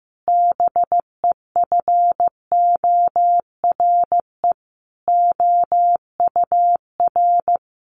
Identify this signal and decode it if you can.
Morse code, heard as a keyed tone, about 15 wpm, 700 Hz. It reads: BEFORE OUR
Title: A Sound Of Thunder 15wpm